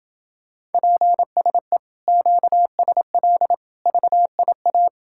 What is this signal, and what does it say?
Morse code, heard as a keyed tone, about 30 wpm, 700 Hz. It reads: PSE QSL VIA